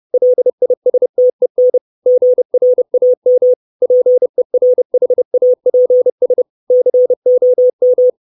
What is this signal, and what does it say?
Morse code, heard as a keyed tone, about 30 wpm, 500 Hz. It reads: LISTEN GRAM PERHAPS COM
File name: 632 Words 30WPM